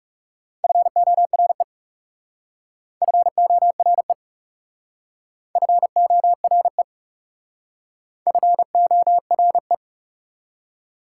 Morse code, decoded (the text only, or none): FORE FORE FORE FORE